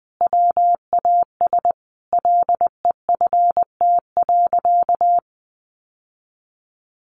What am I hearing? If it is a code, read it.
Morse code, heard as a keyed tone, about 20 wpm, 700 Hz. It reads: WAS LEFT.
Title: Sentences from Top 1200 Words 20wpm